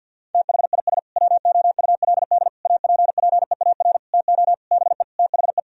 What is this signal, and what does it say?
Morse code, heard as a keyed tone, about 46 wpm, 700 Hz. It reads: THIS WOULD APPEAR TO BE THE
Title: Sentences from Top 1000 Words - Review Part 3 - 50wpm